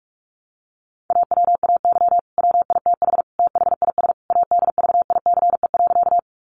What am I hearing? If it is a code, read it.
Morse code, heard as a keyed tone, about 44 wpm, 700 Hz. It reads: AWAY WITH THIS ADVICE.
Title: Sentences from Top 1100 Words 45wpm